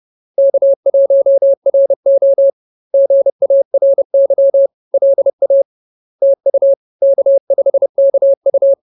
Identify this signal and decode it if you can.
Morse code, heard as a keyed tone, about 30 wpm, 550 Hz. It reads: K1RO GARY LA TU K5KU